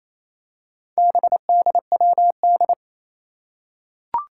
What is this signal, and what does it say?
Morse code, heard as a keyed tone, about 28 wpm, 700 Hz. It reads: BDWD
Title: Four Challenging Letter-Number Combinations 28wpm